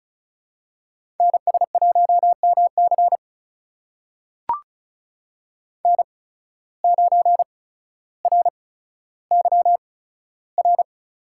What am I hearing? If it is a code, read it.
Morse code, heard as a keyed tone, about 36 wpm, spaced as for 15 wpm, 700 Hz. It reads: NS1MC N9RYR
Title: USA Call Signs - 8x Character Spacing - 35wpm